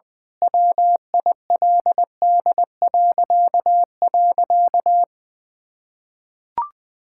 Morse code, heard as a keyed tone, about 20 wpm, 700 Hz. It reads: WILD..
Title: Pangram Challenge v2 - 20wpm